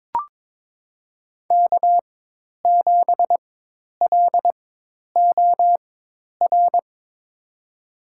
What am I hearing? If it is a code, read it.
Morse code, heard as a keyed tone, about 22 wpm, spaced as for 10 wpm, 700 Hz. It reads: K7LOR
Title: USA Call Signs - 4x Character Spacing - 22wpm